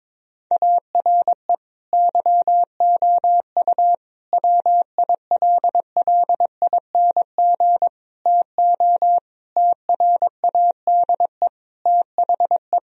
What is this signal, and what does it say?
Morse code, heard as a keyed tone, about 22 wpm, 700 Hz. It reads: ARE YOU WILLING TO TRADE THE
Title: Sentences from Top 1000 Words - Review Part 4 - 22wpm